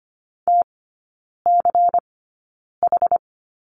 Morse code, heard as a keyed tone, about 25 wpm, spaced as for 9 wpm, 700 Hz. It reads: TCH
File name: course / 25:10wpm / 093 Lesson 16e - Letter c - Three Characters 25wpm